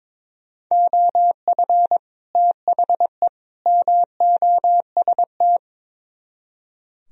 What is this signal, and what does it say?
Morse code, heard as a keyed tone, about 24 wpm, 700 Hz. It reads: OF THE MOST
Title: Speed-Racing - Sets of 3 Words 22wpm